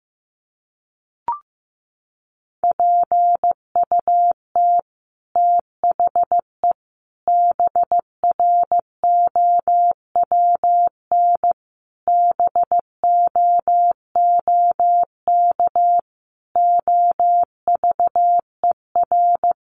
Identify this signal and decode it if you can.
Morse code, heard as a keyed tone, about 15 wpm, 700 Hz. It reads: PUT THE BROWN BOOK OVER
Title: Sentences from Top 1200 Words 15wpm